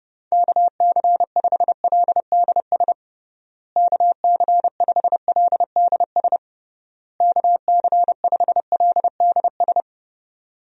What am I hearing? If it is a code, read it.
Morse code, heard as a keyed tone, about 30 wpm, 700 Hz. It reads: KC5LDS KC5LDS KC5LDS
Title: USA Call Signs - 6 Characters - Repeated 3x - 30wpm